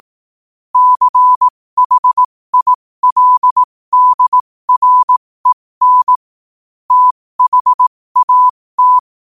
Morse code, heard as a keyed tone, about 18 wpm, spaced as for 15 wpm, 1000 Hz. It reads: CHILDREN THAT
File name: qotdfiles / 15wpm